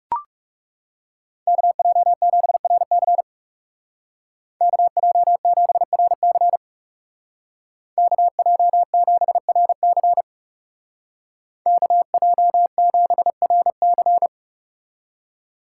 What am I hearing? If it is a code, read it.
Morse code, heard as a keyed tone, about 40 wpm, 700 Hz. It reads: KJ7RC KJ7RC KJ7RC KJ7RC